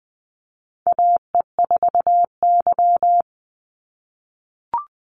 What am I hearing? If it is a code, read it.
Morse code, heard as a keyed tone, about 20 wpm, 700 Hz. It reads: AE4Y
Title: NA SOTA Chaser Callsigns 20wpm